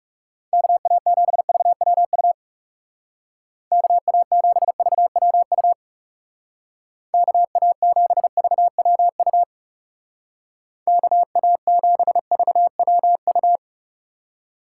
Morse code, heard as a keyed tone, about 37 wpm, 700 Hz. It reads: KA7VWU KA7VWU KA7VWU KA7VWU
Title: Speed Racing - USA 2x3 Call Signs 30wpm